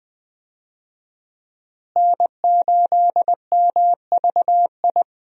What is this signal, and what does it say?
Morse code, heard as a keyed tone, about 20 wpm, 700 Hz. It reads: N8MVI
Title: USA Call Signs - 5 Characters - Repeated 2x - 20wpm